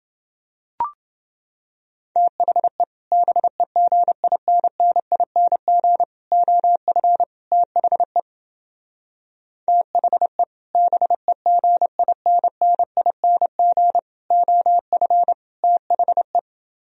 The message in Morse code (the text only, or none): THE BEGINNING OF THE THE BEGINNING OF THE